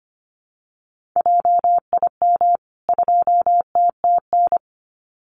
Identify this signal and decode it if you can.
Morse code, heard as a keyed tone, about 25 wpm, 700 Hz. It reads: JIM 2TTN
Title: CWT Exchange 25wpm